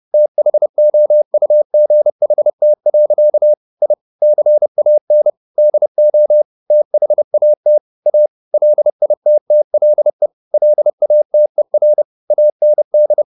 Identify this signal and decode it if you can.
Morse code, heard as a keyed tone, about 30 wpm, 600 Hz. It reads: THOUGHT. I CAN DO THAT A LITTLE LATER AND